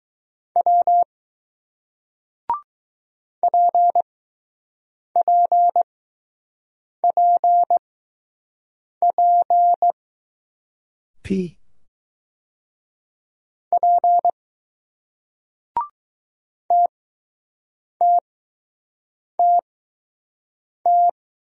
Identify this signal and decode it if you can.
Morse code, heard as a keyed tone, about 20 wpm, spaced as for 6 wpm, 700 Hz. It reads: W PPPP P TTTT